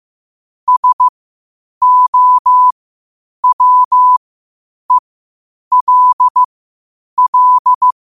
Morse code, heard as a keyed tone, about 15 wpm, spaced as for 8 wpm, 1000 Hz. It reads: SOWELL